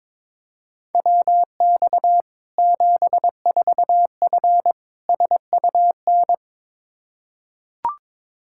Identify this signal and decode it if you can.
Morse code, heard as a keyed tone, about 22 wpm, 700 Hz. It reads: WX 74F SUN
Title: Practice QSO - Exchange 2 - Info - 22wpm